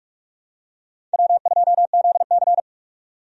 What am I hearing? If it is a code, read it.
Morse code, heard as a keyed tone, about 46 wpm, 700 Hz. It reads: W2ZC
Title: USA Call Signs - 4 Characters - 45wpm